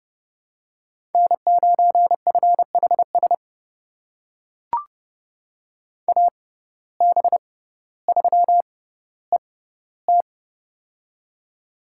Morse code, heard as a keyed tone, about 30 wpm, spaced as for 15 wpm, 700 Hz. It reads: N9FHS AB3ET
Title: USA Call Signs - 5 Characters - 6x Character Spacing - 30wpm